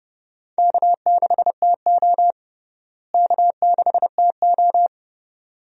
Morse code, heard as a keyed tone, about 32 wpm, 700 Hz. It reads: K6TO K6TO
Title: USA Call Signs - 4 Characters - Repeated 2x - 30wpm